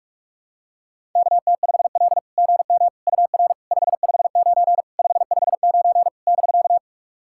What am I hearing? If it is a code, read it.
Morse code, heard as a keyed tone, about 44 wpm, 690 Hz. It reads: KT5L GM UR 559 559 <BK>
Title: SOTA QSO Practice 45wpm